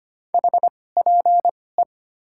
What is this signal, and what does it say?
Morse code, heard as a keyed tone, about 25 wpm, spaced as for 18 wpm, 700 Hz. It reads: HPE
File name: Top 90 QSO Elements - 2x Character Spacing - 25wpm